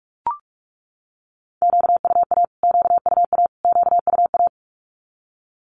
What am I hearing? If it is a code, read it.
Morse code, heard as a keyed tone, about 44 wpm, 700 Hz. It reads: QUA QUA QUA
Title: Q-Codes Repeated 3x 45wpm